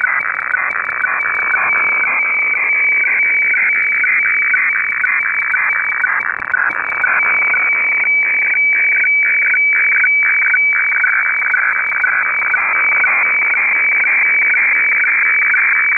Weather fax (radiofax).